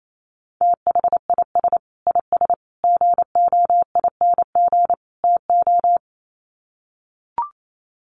Morse code, heard as a keyed tone, about 28 wpm, 700 Hz. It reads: THIS IS GOING TO
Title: Sets of 4 Words 28wpm